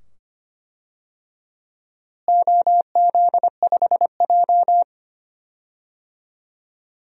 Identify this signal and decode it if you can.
Morse code, heard as a keyed tone, about 25 wpm, 700 Hz. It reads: OZ5J
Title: DXCC-100 Call Signs - 4 Characters - Repeated 2x - 25wpm